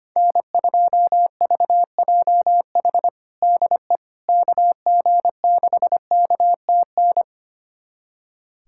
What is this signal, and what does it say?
Morse code, heard as a keyed tone, about 25 wpm, 700 Hz. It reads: N2VJH DE KG6KTN
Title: USA Callsign DE Callsign 25wpm